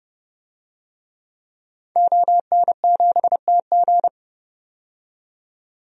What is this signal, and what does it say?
Morse code, heard as a keyed tone, about 31 wpm, 700 Hz. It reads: ON7TG
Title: DXCC-100 Call Signs - 5 Characters - 30wpm